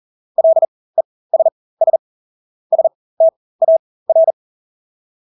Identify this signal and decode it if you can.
Morse code, heard as a keyed tone, about 40 wpm, spaced as for 20 wpm, 650 Hz. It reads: LESS STAR
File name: Engelske ord 40-20 wpm